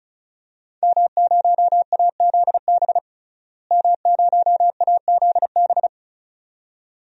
Morse code, heard as a keyed tone, about 36 wpm, 700 Hz. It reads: M0AZB M0AZB
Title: DXCC-100 Call Signs - 5 Characters - Repeated 2x - 35wpm